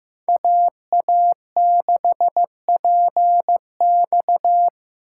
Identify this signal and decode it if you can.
Morse code, heard as a keyed tone, about 18 wpm, 700 Hz. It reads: AA6PX
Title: Speed Racing - USA Call Signs 15wpm